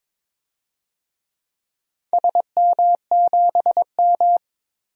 Morse code, heard as a keyed tone, about 22 wpm, 700 Hz. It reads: SM7M
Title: DXCC-100 Call Signs - 4 Characters - 22wpm